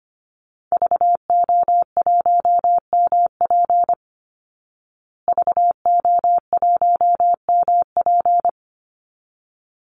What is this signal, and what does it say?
Morse code, heard as a keyed tone, about 25 wpm, 700 Hz. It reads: VO1MP VO1MP
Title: Call Signs Repeated 3x - 25wpm